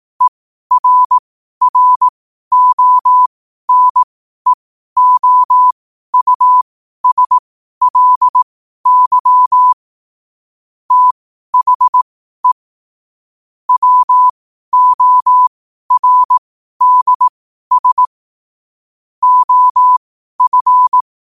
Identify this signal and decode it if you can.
Morse code, heard as a keyed tone, about 18 wpm, spaced as for 12 wpm, 1000 Hz. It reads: ERRONEOUSLY THE WORDS OF
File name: qotdfiles / 12wpm